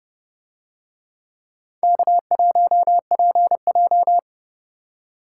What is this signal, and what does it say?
Morse code, heard as a keyed tone, about 30 wpm, 700 Hz. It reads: K1PJ